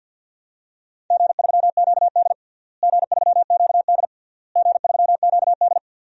Morse code, heard as a keyed tone, about 46 wpm, 700 Hz. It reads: G3QD G3QD G3QD
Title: DXCC-100 Call Signs - 4 Characters - Repeated 3x - 50wpm